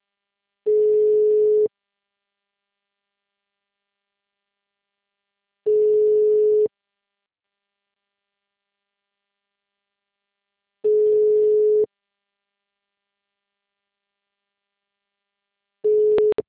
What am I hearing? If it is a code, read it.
Ringback tone (Europe, 425 Hz).
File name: out- -2018- - - .56641